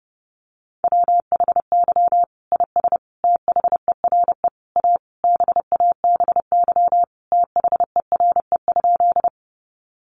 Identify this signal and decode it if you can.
Morse code, heard as a keyed tone, about 30 wpm, 700 Hz. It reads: WHY IS THERE A BABY THERE?
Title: Sentences from Top 1300 Words 30wpm